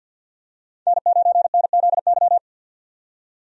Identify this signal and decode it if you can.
Morse code, heard as a keyed tone, about 48 wpm, 700 Hz. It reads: N9NZY
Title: USA Call Signs - 5 Characters - 50wpm